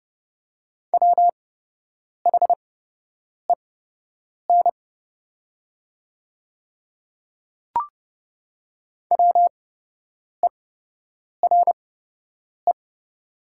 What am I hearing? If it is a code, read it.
Morse code, heard as a keyed tone, about 30 wpm, spaced as for 8 wpm, 700 Hz. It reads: WHEN WERE
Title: 8x Character Spacing - Top 100 Words - 30wpm